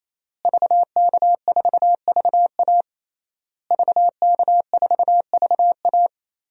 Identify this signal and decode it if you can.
Morse code, heard as a keyed tone, about 28 wpm, 700 Hz. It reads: VK4VA VK4VA
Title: DXCC-100 Call Signs - 5 Characters - Repeated 3x - 28wpm